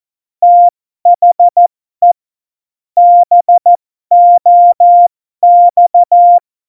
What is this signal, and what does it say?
Morse code, heard as a keyed tone, about 14 wpm, spaced as for 12 wpm, 700 Hz. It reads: THE BOX
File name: Morsekurs / Witze